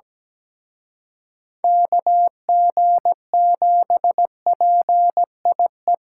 Morse code, heard as a keyed tone, about 20 wpm, 700 Hz. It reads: KG7PIE
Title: Speed Racing - USA 2x3 Call Signs 17wpm